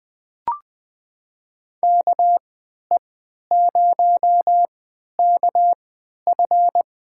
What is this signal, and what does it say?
Morse code, heard as a keyed tone, about 20 wpm, spaced as for 11 wpm, 700 Hz. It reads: KE0KF
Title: USA Call Signs - 3x Character Spacing - 20wpm